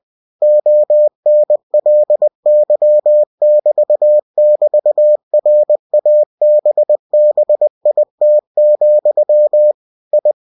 Morse code, heard as a keyed tone, about 20 wpm, 600 Hz. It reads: ONLY<BT><BT>RABBIT, I
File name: cw-books / winnie-the-pooh / winnie-20wpm-page-0084